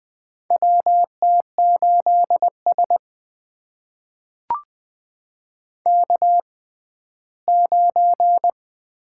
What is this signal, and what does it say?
Morse code, heard as a keyed tone, about 20 wpm, spaced as for 14 wpm, 700 Hz. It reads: WT8S K9
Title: USA Call Signs - 4 Characters - 6x Character Spacing - 20wpm